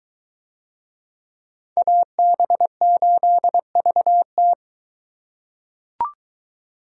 Morse code, heard as a keyed tone, about 23 wpm, 700 Hz. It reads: AB8VT